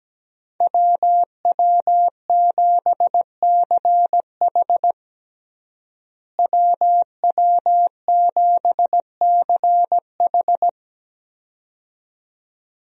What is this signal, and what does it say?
Morse code, heard as a keyed tone, about 17 wpm, 700 Hz. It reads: WW7CH WW7CH
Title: USA Call Signs - 5 Characters - Repeated 2x - 17wpm